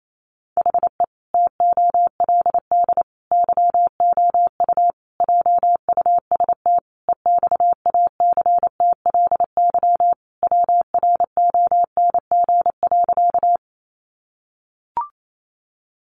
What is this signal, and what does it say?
Morse code, heard as a keyed tone, about 28 wpm, 700 Hz. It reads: HE TOLD YOU JUST EXACTLY WRONG.
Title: Sentences from Top 900 Words 28wpm